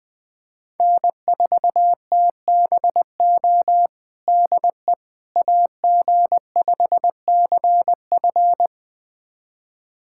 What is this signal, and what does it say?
Morse code, heard as a keyed tone, about 20 wpm, 700 Hz. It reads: N4TBO DE AG5CF
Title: USA Callsign DE Callsign - 5 Characters - 20wpm